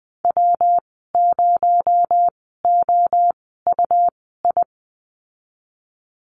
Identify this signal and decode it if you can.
Morse code, heard as a keyed tone, about 20 wpm, spaced as for 14 wpm, 700 Hz. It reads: W0OUI
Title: 219 Lesson 35g - Number 0 - Call Signs 20wpm